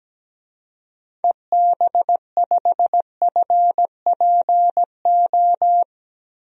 Morse code, heard as a keyed tone, about 17 wpm, 700 Hz. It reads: EB5FPO
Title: DXCC-100 Call Signs - 6 Characters - Repeated 2x - 17wpm